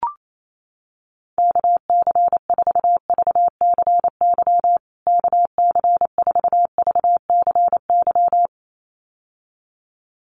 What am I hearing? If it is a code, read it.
Morse code, heard as a keyed tone, about 28 wpm, 700 Hz. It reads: KC4VCY KC4VCY
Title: USA Call Signs Repeated 2x 28wpm